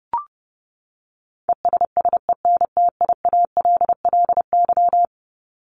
Morse code, heard as a keyed tone, about 30 wpm, 700 Hz. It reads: ESSENTIALLY
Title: LY Suffix - Top 100 Words with Base Words 30wpm